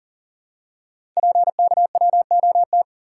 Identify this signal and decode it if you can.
Morse code, heard as a keyed tone, about 39 wpm, 700 Hz. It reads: PKWOT
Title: Code Groups - 5 Letters - 40wpm